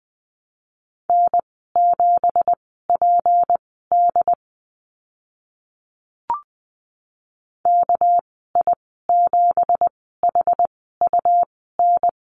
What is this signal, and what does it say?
Morse code, heard as a keyed tone, about 20 wpm, spaced as for 14 wpm, 700 Hz. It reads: N7PD KI7HUN
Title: 213 Lesson 34g - Number 8 - Call Signs 20wpm